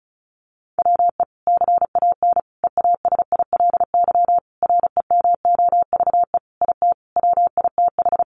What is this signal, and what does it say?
Morse code, heard as a keyed tone, about 34 wpm, 700 Hz. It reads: WE CAN EASILY REMOVE IT WITH
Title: Sentences from Top 1200 Words 35wpm